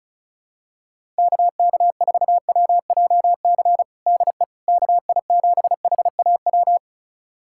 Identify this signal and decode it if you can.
Morse code, heard as a keyed tone, about 34 wpm, 700 Hz. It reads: KK4WJC DE KI7HAW
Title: USA Callsign DE Callsign - 6 Characters - 35wpm